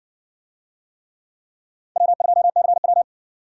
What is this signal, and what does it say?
Morse code, heard as a keyed tone, about 57 wpm, 700 Hz. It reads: W2CW